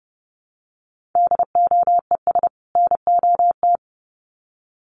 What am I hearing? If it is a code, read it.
Morse code, heard as a keyed tone, about 30 wpm, 700 Hz. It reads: DOES NOT
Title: Sets of 2 Words 30wpm